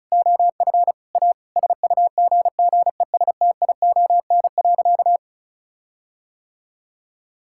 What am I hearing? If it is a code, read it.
Morse code, heard as a keyed tone, about 34 wpm, 700 Hz. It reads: OF A SUGGESTION.
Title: Sentences from Top 1600 Words 35wpm